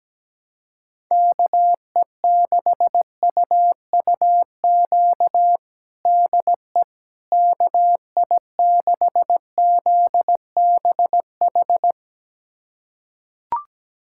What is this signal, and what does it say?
Morse code, heard as a keyed tone, about 17 wpm, 700 Hz. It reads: KE6UUQ DE KI6ZBH
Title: USA Callsign DE Callsign 17wpm